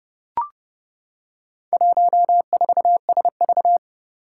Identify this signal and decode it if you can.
Morse code, heard as a keyed tone, about 31 wpm, 700 Hz. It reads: 14SV
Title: Four Challenging Letter-Number Combinations 30wpm